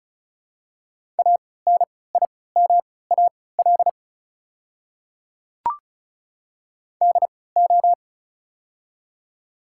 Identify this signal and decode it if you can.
Morse code, heard as a keyed tone, about 36 wpm, spaced as for 20 wpm, 700 Hz. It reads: ANIMAL DO
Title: 3x Character Spacing - Top 200 Words - 35wpm